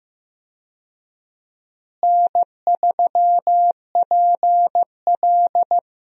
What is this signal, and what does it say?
Morse code, heard as a keyed tone, about 15 wpm, 700 Hz. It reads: N3PL
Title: USA Call Signs - 4 Characters - Repeated 3x - 15wpm